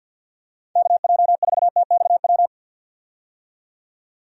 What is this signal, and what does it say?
Morse code, heard as a keyed tone, about 51 wpm, 700 Hz. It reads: KJ4TXW
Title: USA Call Signs - 6 Characters - 50wpm